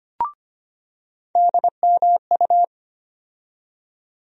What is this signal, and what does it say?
Morse code, heard as a keyed tone, about 25 wpm, 700 Hz. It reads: DMU
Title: USA Call Sign Postfixes - 3 Letters 25wpm